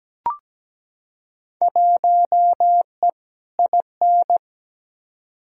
Morse code, heard as a keyed tone, about 17 wpm, 700 Hz. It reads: 1E IN
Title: Field Day Exchanges Encore - 17wpm